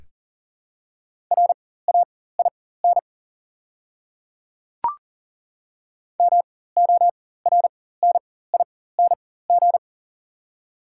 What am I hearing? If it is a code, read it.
Morse code, heard as a keyed tone, about 40 wpm, spaced as for 19 wpm, 690 Hz. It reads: RAIN MORNING
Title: 4x Character Spacing - Top 500 Words - 40wpm